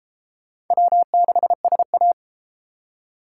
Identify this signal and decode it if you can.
Morse code, heard as a keyed tone, about 31 wpm, 700 Hz. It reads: W6SA